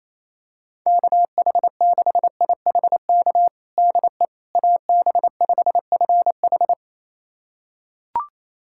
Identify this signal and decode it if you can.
Morse code, heard as a keyed tone, about 28 wpm, 700 Hz. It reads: KH6IHK DE AB5FH
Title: USA Callsign DE Callsign 28wpm